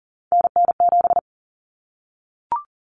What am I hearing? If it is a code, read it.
Morse code, heard as a keyed tone, about 40 wpm, 700 Hz. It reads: NN7
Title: Common Cut Numbers - 3-Digit 40wpm